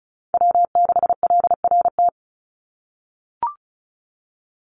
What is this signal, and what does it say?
Morse code, heard as a keyed tone, about 34 wpm, 700 Hz. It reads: W6LRT
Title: USA Call Signs 35wpm